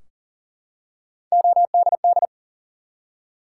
Morse code, heard as a keyed tone, about 38 wpm, 700 Hz. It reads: ODD
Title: Three Letter Words - 40wpm